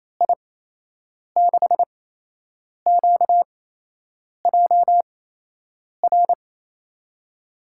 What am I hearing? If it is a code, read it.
Morse code, heard as a keyed tone, about 27 wpm, spaced as for 8 wpm, 700 Hz. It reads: I6QJR
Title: USA Call Signs - 6 Characters - 8x Character Spacing - 28wpm